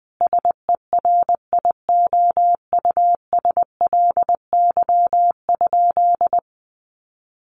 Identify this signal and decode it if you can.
Morse code, heard as a keyed tone, about 20 wpm, 700 Hz. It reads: SERIOUSLY?
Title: Sentences from Top 1200 Words 20wpm